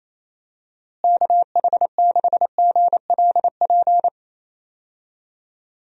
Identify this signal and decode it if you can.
Morse code, heard as a keyed tone, about 28 wpm, 700 Hz. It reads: KH6GLP
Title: DXCC-100 Call Signs - 6 Characters - Repeated 2x - 28wpm